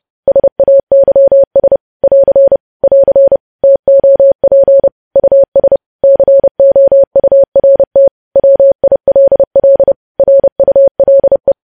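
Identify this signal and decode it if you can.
Morse code, heard as a keyed tone, about 30 wpm, 550 Hz. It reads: SAYS <AR> <AR> TOP US COURT WILL RULE
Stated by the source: Morse news headlines sent at 30 wpm.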